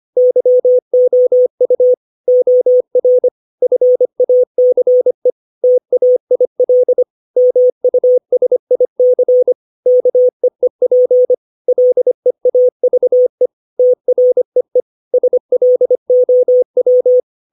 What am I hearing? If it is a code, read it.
Morse code, heard as a keyed tone, about 25 wpm, 500 Hz. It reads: YOU OR FACE TAIL MUSIC KEEP LEAVE TREE SLOW
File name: 632 words / 632 Words 25WPM